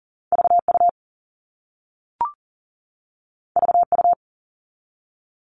Morse code, heard as a keyed tone, about 42 wpm, 700 Hz. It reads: VU VU
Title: All Cut Numbers - 2-Digit 40wpm